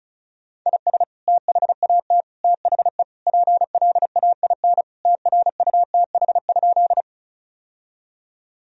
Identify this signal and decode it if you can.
Morse code, heard as a keyed tone, about 34 wpm, 700 Hz. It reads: IS THAT THE PLAIN TRUTH?
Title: Sentences from Top 1000 Words - Review Part 2 - 35wpm